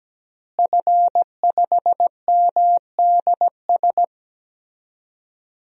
Morse code, heard as a keyed tone, about 17 wpm, 700 Hz. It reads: F5MDS
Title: DXCC-100 Call Signs - 5 Characters - Repeated 3x - 17wpm